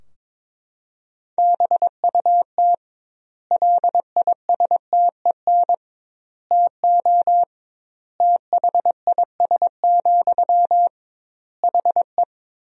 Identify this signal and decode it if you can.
Morse code, heard as a keyed tone, about 22 wpm, 700 Hz. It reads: BUT LISTEN TO THIS, HE
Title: 2x Word-Spacing - Sentences from Top 400 Words - 22wpm